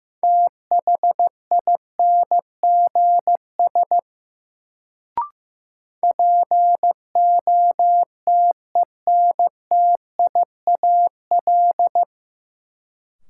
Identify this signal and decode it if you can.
Morse code, heard as a keyed tone, about 15 wpm, 700 Hz. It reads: THINGS POTENTIAL
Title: Sentences from Top 1000 Words - Review Part 4 - New Word Drill - 15wpm